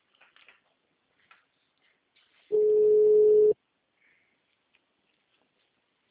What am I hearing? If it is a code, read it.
Ringback tone (Europe, 425 Hz).